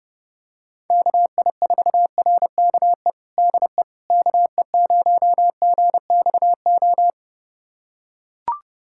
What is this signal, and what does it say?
Morse code, heard as a keyed tone, about 32 wpm, 700 Hz. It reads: KI4RKE DE KE0GXO
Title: USA Callsign DE Callsign - 6 Characters - 30wpm